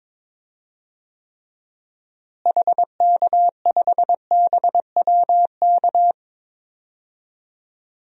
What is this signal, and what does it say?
Morse code, heard as a keyed tone, about 23 wpm, 700 Hz. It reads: HK5BWK
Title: DXCC-100 Call Signs - 6 Characters - Repeated 3x - 22wpm